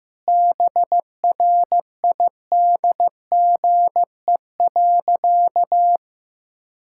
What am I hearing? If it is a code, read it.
Morse code, heard as a keyed tone, about 15 wpm, 700 Hz. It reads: BRIDGE.
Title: Sentences from Top 1600 Words 15wpm